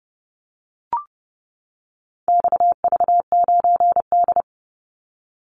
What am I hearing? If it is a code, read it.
Morse code, heard as a keyed tone, about 30 wpm, 700 Hz. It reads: XV9D
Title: Call Signs - 30wpm